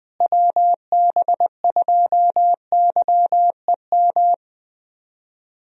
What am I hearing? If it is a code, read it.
Morse code, heard as a keyed tone, about 20 wpm, 700 Hz. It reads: WB2YEM
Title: USA Call Signs - 6 Characters - Repeated 3x - 20wpm